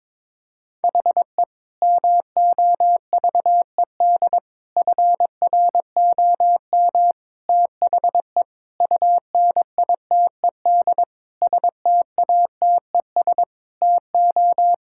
Morse code, heard as a keyed tone, about 22 wpm, 700 Hz. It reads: HE MOVED FROM THE UNITED STATES TO
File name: practice / 22wpm / Sentences with 4-Word Phrases - 22wpm